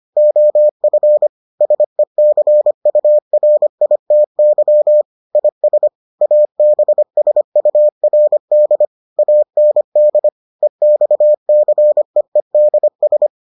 Morse code, heard as a keyed tone, about 25 wpm, 600 Hz. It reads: OF SECURITY IS ABSURD AND EXCEEDS